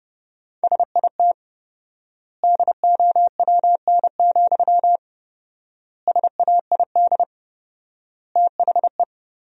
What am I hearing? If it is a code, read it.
Morse code, heard as a keyed tone, about 30 wpm, 700 Hz. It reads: SIT DOWN, SAID THE
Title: 4x Word-Spacing - Sentences from Top 500 Words - 30wpm